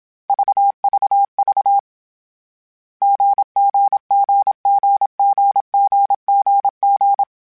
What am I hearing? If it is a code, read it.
Morse code, heard as a keyed tone, about 27 wpm, 800 Hz. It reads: VVV GGGGGGGG
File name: CW Code Groups-mp3s / Sets mp3s / GZ GZ7 sp @25